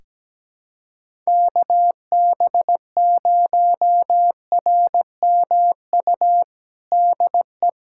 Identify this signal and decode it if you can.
Morse code, heard as a keyed tone, about 17 wpm, 700 Hz. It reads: KB0RMU DE
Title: USA Callsign DE Callsign - 6 Characters - 17wpm